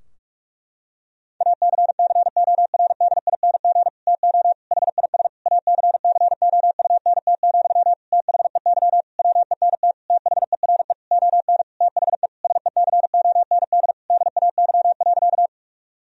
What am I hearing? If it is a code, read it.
Morse code, heard as a keyed tone, about 46 wpm, 700 Hz. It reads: ACCORDING TO HIS ACCOUNT, THEY WENT THERE ON THE SECOND DAY.
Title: Sentences from Top 1000 Words - Review Part 4 - 45wpm